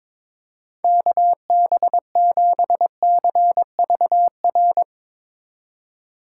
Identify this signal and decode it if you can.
Morse code, heard as a keyed tone, about 22 wpm, 700 Hz. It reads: KB7CVR
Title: USA Call Signs - 6 Characters - 4x Character Spacing - 22wpm